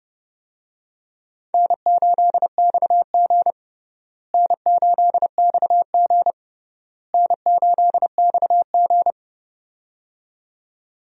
Morse code, heard as a keyed tone, about 31 wpm, 700 Hz. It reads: N8XG N8XG N8XG
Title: USA Call Signs - 4 Characters - Repeated 3x - 30wpm